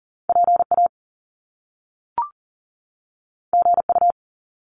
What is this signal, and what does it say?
Morse code, heard as a keyed tone, about 41 wpm, 700 Hz. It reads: PA GU
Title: Top Two Letter Combinations in Words 40wpm